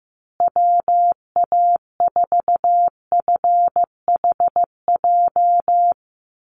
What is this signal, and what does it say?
Morse code, heard as a keyed tone, about 15 wpm, 700 Hz. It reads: WA4FHJ
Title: USA Call Signs Repeated 2x 15wpm